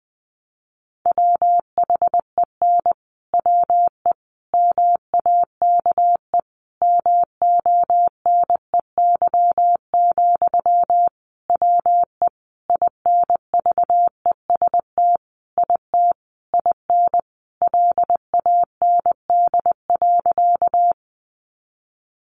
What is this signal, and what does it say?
Morse code, heard as a keyed tone, about 20 wpm, 700 Hz. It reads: WHEN WE MAKE MONEY, WE INVEST IT IN LAND.
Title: Sentences from Top 1100 Words 20wpm